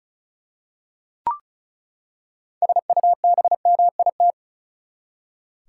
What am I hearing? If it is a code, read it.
Morse code, heard as a keyed tone, about 36 wpm, 700 Hz. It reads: SUBMIT
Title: Six Letter Words - 35wpm